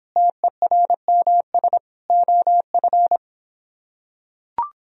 Morse code, heard as a keyed tone, about 26 wpm, 700 Hz. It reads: TERMS OF